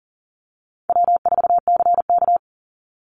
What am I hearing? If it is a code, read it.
Morse code, heard as a keyed tone, about 38 wpm, 700 Hz. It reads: W4CK
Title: Easy Call Signs - 40wpm